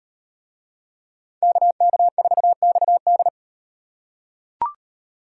Morse code, heard as a keyed tone, about 38 wpm, 700 Hz. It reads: KK4XD